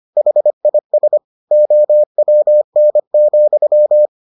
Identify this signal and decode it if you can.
Morse code, heard as a keyed tone, about 25 wpm, 600 Hz. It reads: HIS OWN,